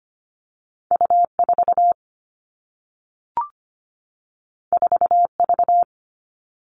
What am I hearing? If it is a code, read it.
Morse code, heard as a keyed tone, about 25 wpm, 700 Hz. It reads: U 4 4 V
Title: All Cut Numbers - 2-Digit 25wpm